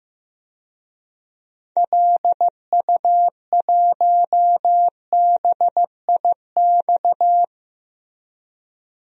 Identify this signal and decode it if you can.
Morse code, heard as a keyed tone, about 15 wpm, 700 Hz. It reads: LU1BIX
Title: DXCC-100 Call Signs - 6 Characters - Repeated 3x - 15wpm